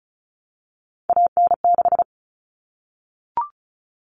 Morse code, heard as a keyed tone, about 34 wpm, 700 Hz. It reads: AN6
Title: All Cut Numbers - 3-Digit 35wpm